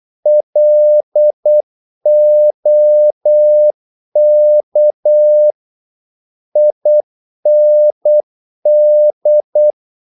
Morse code, heard as a keyed tone, about 8 wpm, 600 Hz. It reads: LOK IND
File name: Korte ord DK 8wpm